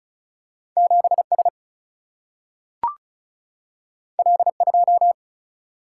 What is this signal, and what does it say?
Morse code, heard as a keyed tone, about 36 wpm, 700 Hz. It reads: 7S L2
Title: Single-Letter Single-Number 35wpm